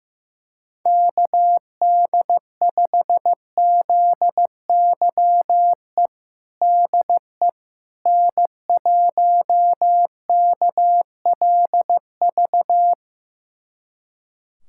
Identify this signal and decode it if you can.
Morse code, heard as a keyed tone, about 15 wpm, 700 Hz. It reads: KD5ZYE DE N1KLV
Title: USA Callsign DE Callsign 15wpm